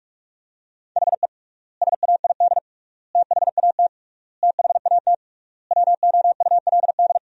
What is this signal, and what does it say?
Morse code, heard as a keyed tone, about 47 wpm, 700 Hz. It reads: HE SAID THAT THAT WOULD
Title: 3x Word-Spacing - Sentences from Top 200 Words - 45wpm